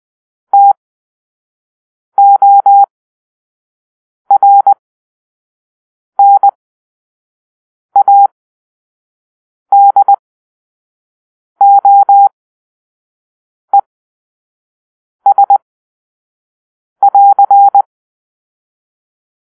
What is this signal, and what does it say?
Morse code, heard as a keyed tone, about 20 wpm, spaced as for 5 wpm, 800 Hz. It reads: TORNADOES<AR>
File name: arrl / Grp-2 Drill-4